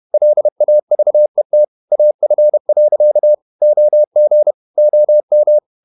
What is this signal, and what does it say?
Morse code, heard as a keyed tone, about 30 wpm, 600 Hz. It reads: LAVET AF. OG OM
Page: Ferie 30 wpm